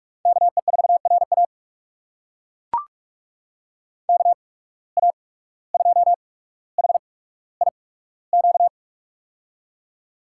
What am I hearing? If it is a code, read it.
Morse code, heard as a keyed tone, about 42 wpm, spaced as for 18 wpm, 700 Hz. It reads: KE4RA KA2HIQ
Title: USA Call Signs - 8x Character Spacing - 45wpm